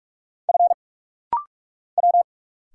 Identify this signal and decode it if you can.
Morse code, heard as a keyed tone, about 44 wpm, 700 Hz. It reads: F W
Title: Single Letter-Number Alt - Rapid-Fire 45wpm